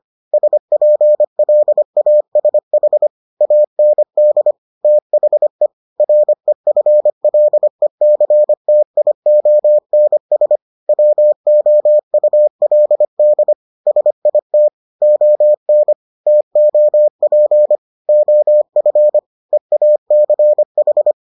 Morse code, heard as a keyed tone, about 25 wpm, 600 Hz. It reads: SPLASH AND THE REFLECTIONS WOULD SIT ON TOP OF EACH